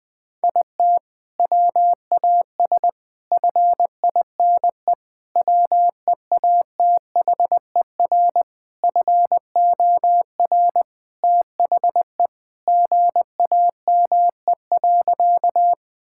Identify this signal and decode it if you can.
Morse code, heard as a keyed tone, about 20 wpm, 700 Hz. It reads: IT WAS FINE WEATHER FOR THE GAME.
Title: Sentences from Top 1000 Words - Review Part 5 - 20wpm